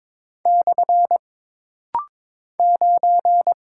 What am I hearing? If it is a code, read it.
Morse code, heard as a keyed tone, about 22 wpm, 700 Hz. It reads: / 9
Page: This rapid-fire timing is geared to push your ICR (Instant Character Recognition) skills.